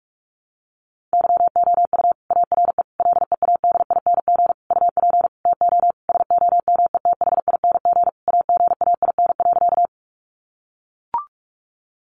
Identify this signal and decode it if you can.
Morse code, heard as a keyed tone, about 44 wpm, 700 Hz. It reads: YOU ARE LEADING UP TO SOMETHING AGAIN.
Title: Sentences from Top 1100 Words 45wpm